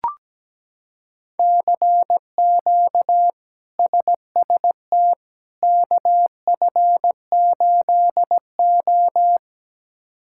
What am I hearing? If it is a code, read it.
Morse code, heard as a keyed tone, about 17 wpm, 700 Hz. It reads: CQ SST KF8O
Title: K1USN SST QSO Practice - 17wpm